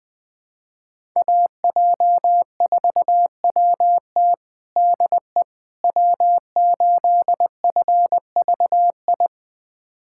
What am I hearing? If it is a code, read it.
Morse code, heard as a keyed tone, about 20 wpm, 700 Hz. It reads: AJ4WT DE W8FVI
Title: USA Callsign DE Callsign - 5 Characters - 20wpm